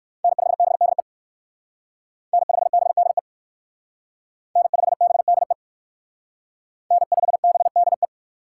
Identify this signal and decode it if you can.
Morse code, heard as a keyed tone, about 58 wpm, 690 Hz. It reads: N5BDE N5BDE N5BDE N5BDE